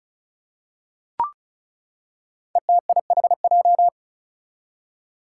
Morse code, heard as a keyed tone, about 36 wpm, 700 Hz. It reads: ETIHJ
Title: Five Challenging Letter-Number Combinations 35wpm